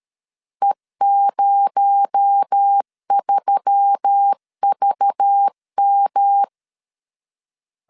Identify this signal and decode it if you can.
Morse code, heard as a keyed tone, about 13 wpm, 800 Hz. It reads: E03VM
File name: LEZIONI / 7 GF9Z / 7 G 12wpm 2055